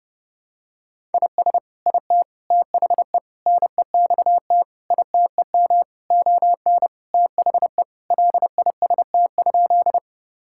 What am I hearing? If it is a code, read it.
Morse code, heard as a keyed tone, about 30 wpm, 700 Hz. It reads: IS IT THE NEXT ITEM ON THE LIST?
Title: Sentences from Top 1000 Words - Review Part 6 - 30wpm